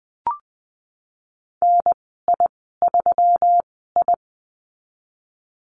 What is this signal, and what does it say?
Morse code, heard as a keyed tone, about 20 wpm, spaced as for 15 wpm, 700 Hz. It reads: NI3I
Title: 173 Lesson 28g - Number 9 - Call Signs 20wpm